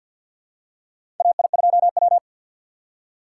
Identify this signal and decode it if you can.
Morse code, heard as a keyed tone, about 48 wpm, 700 Hz. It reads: AI2W
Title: USA Call Signs - 4 Characters - 6x Character Spacing - 50wpm